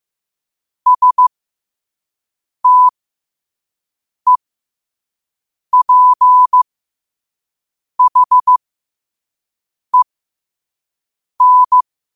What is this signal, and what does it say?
Morse code, heard as a keyed tone, about 15 wpm, spaced as for 5 wpm, 1000 Hz. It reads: STEPHEN
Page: New quotes every day in morse code at 5 Words per minute.